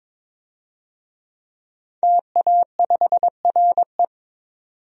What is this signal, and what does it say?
Morse code, heard as a keyed tone, about 22 wpm, 700 Hz. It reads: TA5RE
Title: DXCC-100 Call Signs - 5 Characters - 22wpm